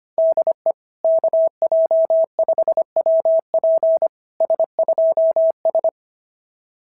Morse code, heard as a keyed tone, about 26 wpm, 650 Hz. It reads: DE KJ5WP S2S
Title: SOTA QSO Practice 25wpm